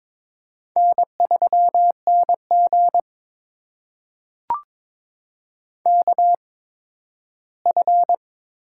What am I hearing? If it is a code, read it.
Morse code, heard as a keyed tone, about 22 wpm, spaced as for 13 wpm, 700 Hz. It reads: N3NG KF
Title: USA Call Signs - 8x Character Spacing - 22wpm